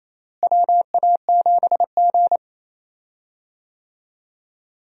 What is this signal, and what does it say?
Morse code, heard as a keyed tone, about 28 wpm, 700 Hz. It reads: WA7G
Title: USA Call Signs - 4 Characters - Repeated 2x - 28wpm